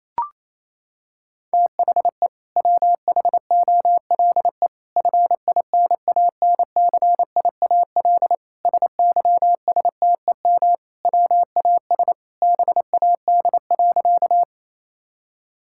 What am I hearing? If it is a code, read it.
Morse code, heard as a keyed tone, about 27 wpm, 700 Hz. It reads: THE WHOLE FINANCIAL SYSTEM WAS BAD.
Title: Sentences from Top 1000 Words - Review Part 5 - 28wpm